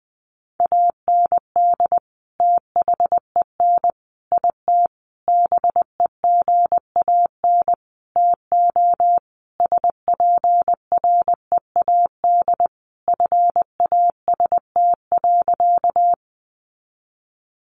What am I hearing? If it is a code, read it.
Morse code, heard as a keyed tone, about 19 wpm, 700 Hz. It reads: AND THEN IT BEGAN TO SPREAD FAST.
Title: Sentences from Top 1200 Words 20wpm